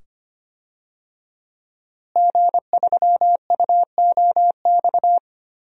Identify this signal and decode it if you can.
Morse code, heard as a keyed tone, about 25 wpm, 700 Hz. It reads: G3UOX
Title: DXCC-100 Call Signs - 5 Characters - Repeated 2x - 25wpm